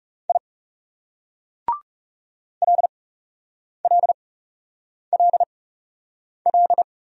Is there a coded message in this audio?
Yes: Morse code from 0.3 to 6.8 s, heard as a keyed tone, about 40 wpm, spaced as for 8 wpm, 700 Hz. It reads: I LLLL